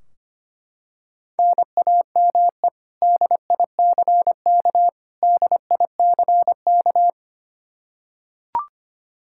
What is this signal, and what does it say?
Morse code, heard as a keyed tone, about 25 wpm, 700 Hz. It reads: NAME DICK DICK
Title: Practice QSO - Exchange 1 - Beginning - 25wpm